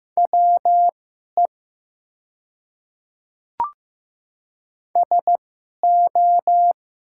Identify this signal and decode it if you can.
Morse code, heard as a keyed tone, about 15 wpm, spaced as for 11 wpm, 700 Hz. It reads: WE SO
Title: 2x Character Spacing - Top 200 Words - 15wpm